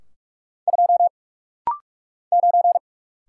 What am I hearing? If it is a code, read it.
Morse code, heard as a keyed tone, about 46 wpm, 700 Hz. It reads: 2 9
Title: Single Letter-Number Alt - Rapid-Fire 45wpm